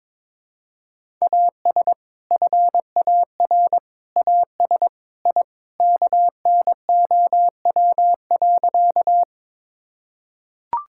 Morse code, heard as a keyed tone, about 22 wpm, 700 Hz. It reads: AS FAR AS I KNOW.
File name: practice / 22wpm / Sentences with 3-Word Phrases - 22wpm